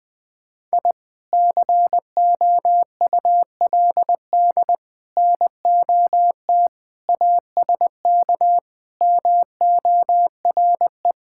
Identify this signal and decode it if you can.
Morse code, heard as a keyed tone, about 21 wpm, 700 Hz. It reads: I COULD NOT ASK MORE
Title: Sentences from Top 1000 Words - Review Part 5 - 20wpm